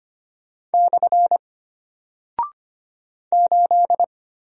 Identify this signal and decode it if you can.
Morse code, heard as a keyed tone, about 25 wpm, 700 Hz. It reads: / 8
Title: Single Character (Letter:Number:Punctuation:Prosign) Alt 25wpm